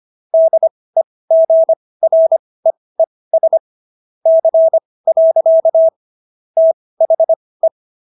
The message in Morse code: DEGREES C. THE

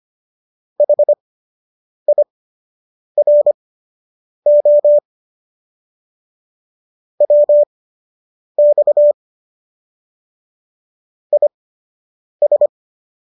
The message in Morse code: HIRO WX IS

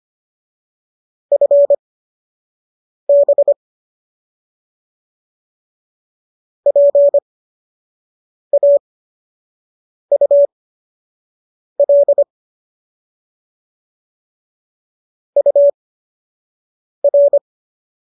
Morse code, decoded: FB PAUL UR

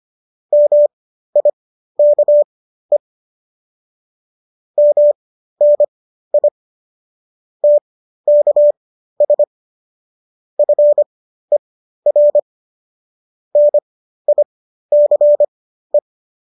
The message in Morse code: MIKE MNI TKS FER NICE

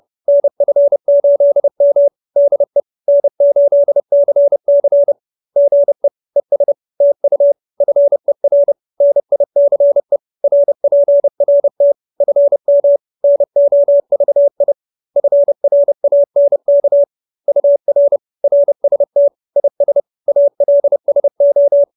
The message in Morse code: NF8M DE N8CC GE ES TU FER NICE RPRT FM NOVI FRANK UR RST IS ALSO

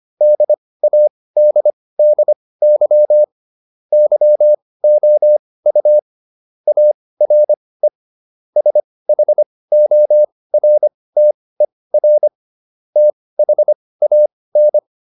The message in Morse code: DADDY YOU ARE SHORTER THAN

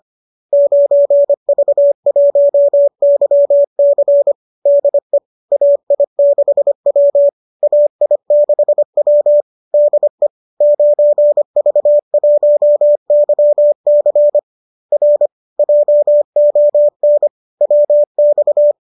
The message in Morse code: 9V1YC DE AI6W AI6W DE 9V1YC R JON WX

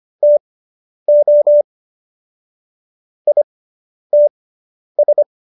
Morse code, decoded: TO ITS

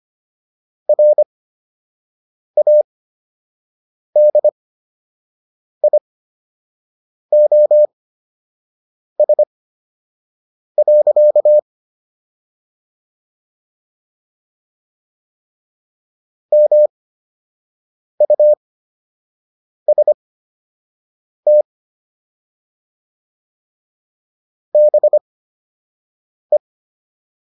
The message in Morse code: RADIOS. MUST BE

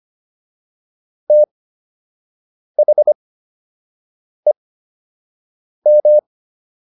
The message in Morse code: THEM

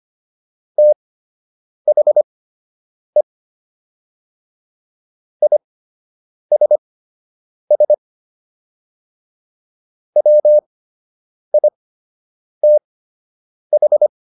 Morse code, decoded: THE ISS WITH

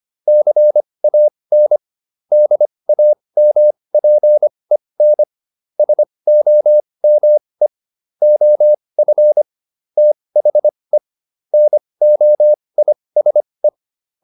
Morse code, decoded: CAN DAMPEN SOME OF THE NOISE

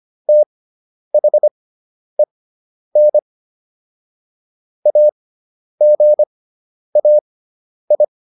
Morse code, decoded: THEN AGAI

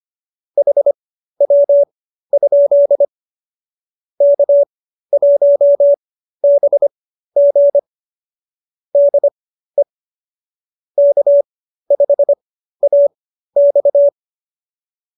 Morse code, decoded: HW? K1BG DE K5AX

